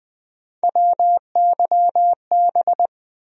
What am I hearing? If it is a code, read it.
Morse code, heard as a keyed tone, about 20 wpm, 700 Hz. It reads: WYB
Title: USA Call Sign Postfixes - 3 Letters 20wpm